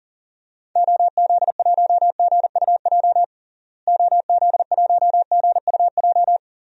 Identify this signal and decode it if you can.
Morse code, heard as a keyed tone, about 40 wpm, 700 Hz. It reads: OZ1GUJ OZ1GUJ
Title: DXCC-100 Call Signs - 6 Characters - Repeated 3x - 40wpm